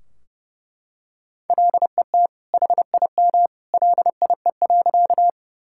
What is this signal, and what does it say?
Morse code, heard as a keyed tone, about 30 wpm, 700 Hz. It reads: LET HIM LIE.
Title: Sentences from Top 1600 Words 30wpm